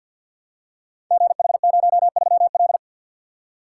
Morse code, heard as a keyed tone, about 46 wpm, 690 Hz. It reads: GH03L
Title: Code Groups - 5 Letter:Numbers - 50wpm